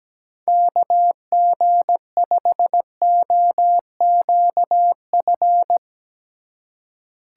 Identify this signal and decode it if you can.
Morse code, heard as a keyed tone, about 17 wpm, 700 Hz. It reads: KG5OQF
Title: USA Call Signs - 6 Characters - 8x Character Spacing - 17wpm